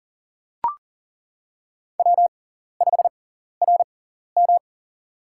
Morse code, heard as a keyed tone, about 40 wpm, spaced as for 14 wpm, 700 Hz. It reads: W5RM
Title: USA Call Signs - 4 Characters - 6x Character Spacing - 40wpm